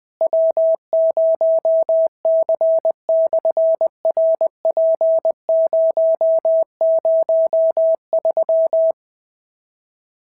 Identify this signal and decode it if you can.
Morse code, heard as a keyed tone, about 20 wpm, 650 Hz. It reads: W0C/RP003
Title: SOTA QSO Practice 20wpm